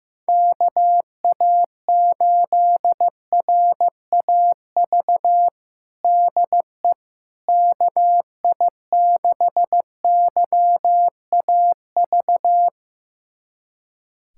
Morse code, heard as a keyed tone, about 15 wpm, 700 Hz. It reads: KA8RAV DE KI6YAV
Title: USA Callsign DE Callsign - 6 Characters - 15wpm